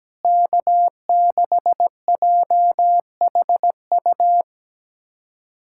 Morse code, heard as a keyed tone, about 17 wpm, 700 Hz. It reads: K6JHU
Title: K1USN SST QSO Practice - 17wpm